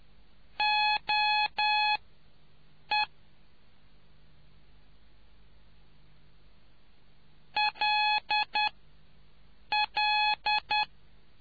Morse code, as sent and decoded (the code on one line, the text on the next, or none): --- . / .-.. .-..
OE LL